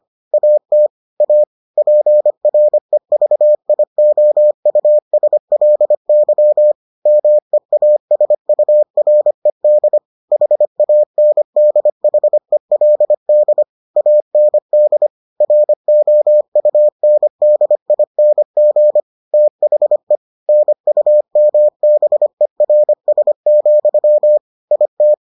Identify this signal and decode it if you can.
Morse code, heard as a keyed tone, about 25 wpm, 600 Hz. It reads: AT A PREVIOUSLY MEASURED HANDHELD AND ROUNDING THE NUMBERS, IT